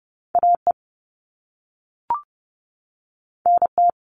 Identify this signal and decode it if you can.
Morse code, heard as a keyed tone, about 30 wpm, 700 Hz. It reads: A E N T
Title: All Cut Numbers - 2-Digit 30wpm